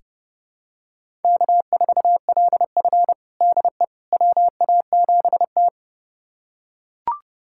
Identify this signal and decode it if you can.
Morse code, heard as a keyed tone, about 30 wpm, 700 Hz. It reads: K4LF DE WA7T
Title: USA Callsign DE Callsign - 4 Characters - 30wpm